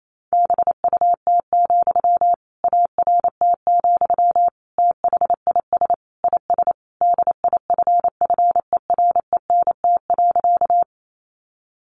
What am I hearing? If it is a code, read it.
Morse code, heard as a keyed tone, about 28 wpm, 700 Hz. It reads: BUT, ART, THIS IS DIFFERENT.
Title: Sentences from Top 600 Words 28wpm